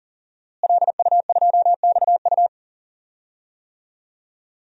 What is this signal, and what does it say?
Morse code, heard as a keyed tone, about 39 wpm, 700 Hz. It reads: LU2XU
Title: DXCC-100 Call Signs - 5 Characters - Repeated 2x - 40wpm